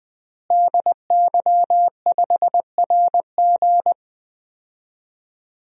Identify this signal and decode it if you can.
Morse code, heard as a keyed tone, about 21 wpm, 700 Hz. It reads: DY5RG
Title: DXCC-100 Call Signs - 5 Characters - Repeated 2x - 20wpm